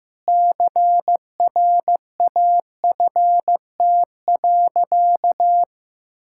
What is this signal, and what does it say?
Morse code, heard as a keyed tone, about 15 wpm, 700 Hz. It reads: CRAFT.
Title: Sentences from Top 1000 Words - Review Part 6 - 15wpm